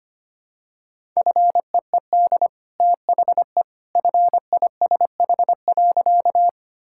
Morse code, heard as a keyed tone, about 25 wpm, 700 Hz. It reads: FEED THE FISH.
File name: Easy Sentences with Hints - 25wpm